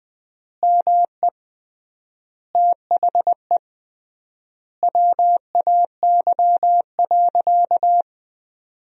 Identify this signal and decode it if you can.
Morse code, heard as a keyed tone, about 20 wpm, 700 Hz. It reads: ME THE WAY.
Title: 3x Word-Spacing - Sentences from Top 200 Words - 20wpm